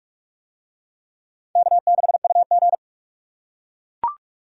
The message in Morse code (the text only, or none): K6UG